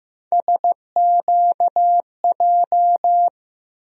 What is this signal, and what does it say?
Morse code, heard as a keyed tone, about 15 wpm, 700 Hz. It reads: SQJ
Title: USA Call Sign Postfixes - 3 Letters 15wpm